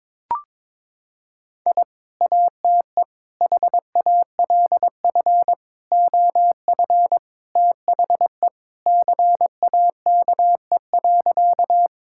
Morse code, heard as a keyed tone, about 22 wpm, 700 Hz. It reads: I ATE HALF OF THE CAKE.
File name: practice / 22wpm / Sentences with 3-Word Phrases - 22wpm